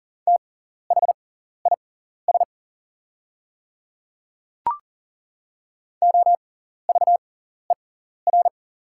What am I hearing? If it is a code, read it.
Morse code, heard as a keyed tone, about 40 wpm, spaced as for 14 wpm, 690 Hz. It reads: THIS OVER
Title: 6x Character Spacing - Top 100 Words - 40wpm